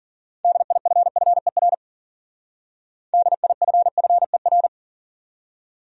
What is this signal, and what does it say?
Morse code, heard as a keyed tone, about 52 wpm, 700 Hz. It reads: DIFFER DIFFER